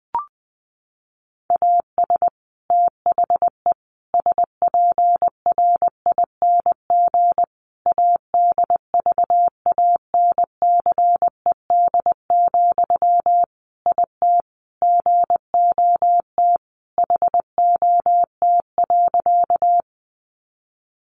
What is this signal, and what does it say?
Morse code, heard as a keyed tone, about 20 wpm, 700 Hz. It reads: AS THE SPRING ADVANCED, IT GOT HOT.
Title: Sentences from Top 1200 Words 20wpm